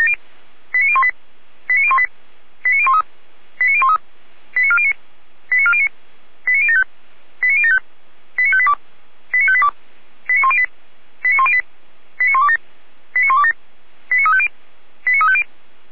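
Five-tone selective call (ZVEI), heard as quick runs of five tones, tones of about 70 ms. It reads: 89018 89018 89013 89013 89409 89409 89075 89075 89582 89582 89108 89108 89127 89127 89350 89350
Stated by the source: ZVEI five-tone selective call received by SDR.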